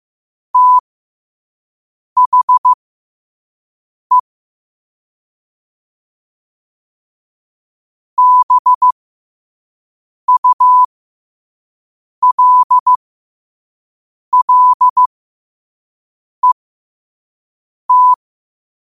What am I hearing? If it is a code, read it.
Morse code, heard as a keyed tone, about 15 wpm, spaced as for 5 wpm, 1000 Hz. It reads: THE BULLET
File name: qotdfiles / 5wpm